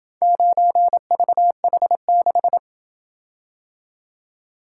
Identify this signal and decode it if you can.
Morse code, heard as a keyed tone, about 27 wpm, 700 Hz. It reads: 9VH6